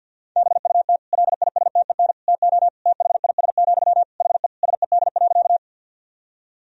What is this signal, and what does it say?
Morse code, heard as a keyed tone, about 51 wpm, 700 Hz. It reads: BUT LISTEN TO THIS, HE SED.
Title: Sentences from Top 400 Words and Abbreviations 50wpm